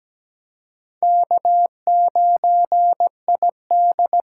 Morse code, heard as a keyed tone, about 17 wpm, 700 Hz. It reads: K9ID
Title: USA Call Signs - 6x Character Spacing - 17wpm